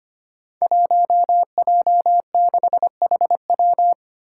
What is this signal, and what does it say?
Morse code, heard as a keyed tone, about 25 wpm, 700 Hz. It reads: 1J6HW
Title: Five Challenging Letter-Number Combinations 25wpm